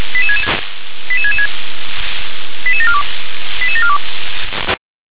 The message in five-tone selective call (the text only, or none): 88666 88742 88742